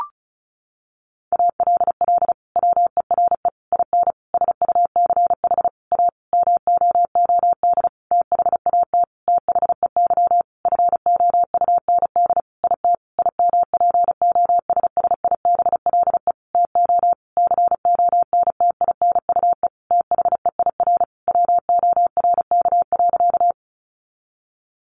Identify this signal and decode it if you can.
Morse code, heard as a keyed tone, about 34 wpm, 700 Hz. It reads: ALL WERE IN SUCH A MOOD THAT THEY FOUND IT IMPOSSIBLE TO CONTINUE THEIR WORK.
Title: Sentences from Top 1100 Words 35wpm